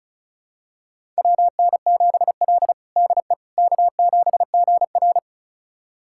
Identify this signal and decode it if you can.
Morse code, heard as a keyed tone, about 36 wpm, 700 Hz. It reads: WN7L DE K7GR
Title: USA Callsign DE Callsign - 4 Characters - 35wpm